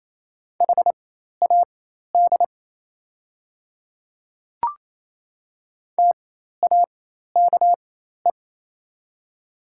Morse code, heard as a keyed tone, about 27 wpm, spaced as for 13 wpm, 690 Hz. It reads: HAD TAKE
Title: 4x Character Spacing - Top 200 Words - 28wpm